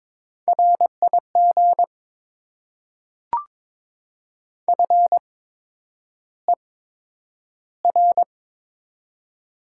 Morse code, heard as a keyed tone, about 22 wpm, spaced as for 9 wpm, 700 Hz. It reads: RIG FER
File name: Top 20 QSO Elements - 8x Character Spacing - 22wpm